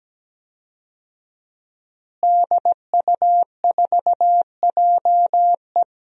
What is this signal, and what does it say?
Morse code, heard as a keyed tone, about 17 wpm, 700 Hz. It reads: DU4JE
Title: DXCC-100 Call Signs - 5 Characters - Repeated 3x - 17wpm